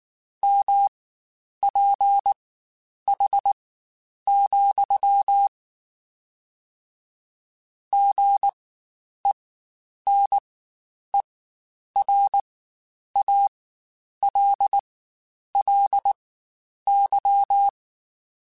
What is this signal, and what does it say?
Morse code, heard as a keyed tone, about 19 wpm, spaced as for 9 wpm, 800 Hz. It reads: MPH, GENERALLY
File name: Spaced Characters / Spaced Characters-mp3s / Weather 5 1 sp @18